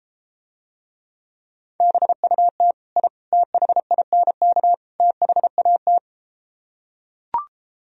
Morse code, heard as a keyed tone, about 33 wpm, 700 Hz. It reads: BUT I THINK THAT